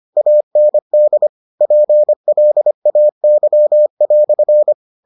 Morse code, heard as a keyed tone, about 25 wpm, 600 Hz. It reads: AND PLAY"